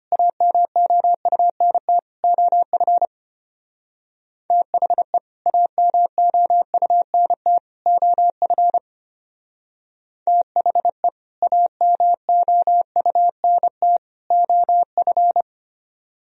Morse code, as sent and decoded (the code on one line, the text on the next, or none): .- -- --- ..- -. - / --- ..-. / - .... / .- -- --- ..- -. - / --- ..-. / - .... / .- -- --- ..- -. - / --- ..-.
AMOUNT OF THE AMOUNT OF THE AMOUNT OF